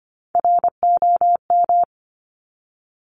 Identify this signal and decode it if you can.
Morse code, heard as a keyed tone, about 25 wpm, 700 Hz. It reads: ROM
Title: Top Three Letter Combinations in Words 25wpm